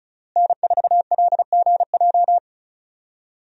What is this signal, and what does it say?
Morse code, heard as a keyed tone, about 34 wpm, 700 Hz. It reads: N4LGJ
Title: USA Call Signs - 5 Characters - Repeated 3x - 35wpm